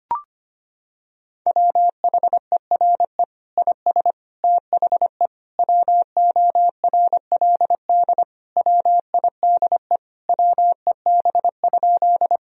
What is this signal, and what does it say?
Morse code, heard as a keyed tone, about 25 wpm, 700 Hz. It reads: WHERE IS THE WORLD WIDE WEB?
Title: Sentences from Top 1000 Words - Review Part 4 - 25wpm